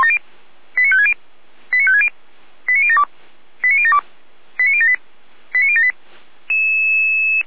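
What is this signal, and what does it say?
Five-tone selective call (ZVEI), heard as quick runs of five tones, tones of about 70 ms. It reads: 89560 89560 89072 89072 89078 89078
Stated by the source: ZVEI five-tone selective call received via SDR, from the Feuerwehr Orlamünde audio archive.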